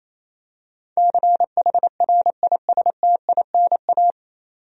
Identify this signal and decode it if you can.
Morse code, heard as a keyed tone, about 28 wpm, 700 Hz. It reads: CHRISTINA
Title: USA Common Names - 28wpm